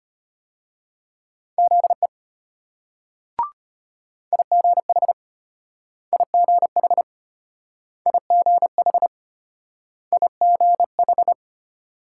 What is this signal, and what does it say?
Morse code, heard as a keyed tone, about 33 wpm, spaced as for 18 wpm, 700 Hz. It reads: ZE IGHIGHIGHIGH